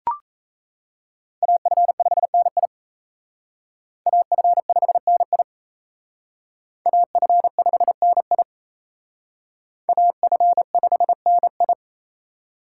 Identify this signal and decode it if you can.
Morse code, heard as a keyed tone, about 35 wpm, 700 Hz. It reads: AF5NI AF5NI AF5NI AF5NI
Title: Speed Racing - USA 2x2 Call Signs 28wpm